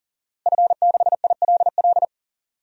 Morse code, heard as a keyed tone, about 40 wpm, 700 Hz. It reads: F6ILL
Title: DXCC-100 Call Signs - 5 Characters - Repeated 2x - 40wpm